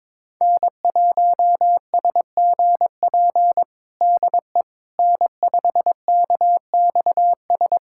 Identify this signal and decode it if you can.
Morse code, heard as a keyed tone, about 22 wpm, 700 Hz. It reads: N1SGP DE N5KXS
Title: USA Callsign DE Callsign - 5 Characters - 22wpm